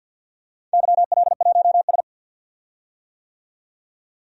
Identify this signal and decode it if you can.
Morse code, heard as a keyed tone, about 48 wpm, 700 Hz. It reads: YL1S
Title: DXCC-100 Call Signs - 4 Characters - 50wpm